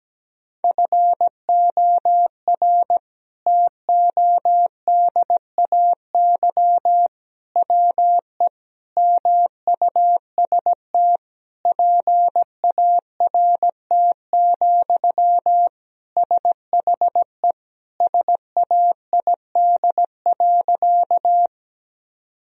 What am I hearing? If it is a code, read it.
Morse code, heard as a keyed tone, about 17 wpm, 700 Hz. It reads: FOR TODAY WE MUST PART, SHE SAID.
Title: Sentences from Top 1000 Words - Review Part 1 - 17wpm